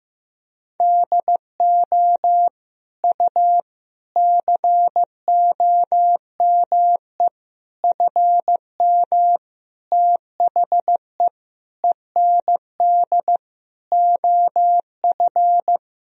Morse code, heard as a keyed tone, about 16 wpm, 700 Hz. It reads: DO U COME FM THE END OF
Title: Sentences from Top 200 Words and Abbreviations 15wpm